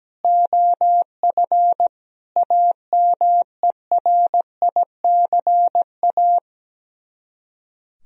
Morse code, heard as a keyed tone, about 17 wpm, 700 Hz. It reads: OF AMERICA
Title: Speed-Racing - Sets of 5 Words 17wpm